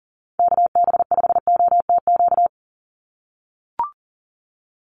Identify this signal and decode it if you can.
Morse code, heard as a keyed tone, about 40 wpm, 700 Hz. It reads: KB5OTQ
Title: USA Call Signs 40wpm